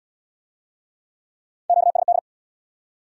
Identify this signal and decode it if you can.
Morse code, heard as a keyed tone, about 72 wpm, 700 Hz. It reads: QSD